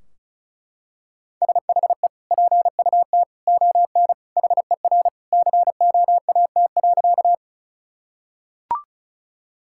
Morse code, heard as a keyed tone, about 36 wpm, 700 Hz. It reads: SHE PUT ON HER COAT.
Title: Sentences from Top 1600 Words 35wpm